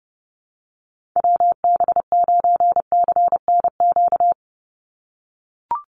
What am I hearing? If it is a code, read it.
Morse code, heard as a keyed tone, about 30 wpm, 700 Hz. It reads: WB9CNQ
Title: USA Call Signs 30wpm